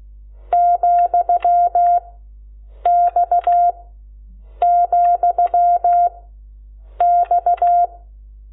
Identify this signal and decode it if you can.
Morse code, heard as a keyed tone, about 16 wpm, 700 Hz. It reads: , X , X
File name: Lecciones / CLASE Nº12